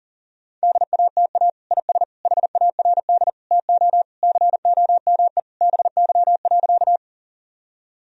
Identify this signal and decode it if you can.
Morse code, heard as a keyed tone, about 40 wpm, 700 Hz. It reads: DATA IS HARD TO COME BY.
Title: Sentences from Top 1000 Words - Review Part 3 - 40wpm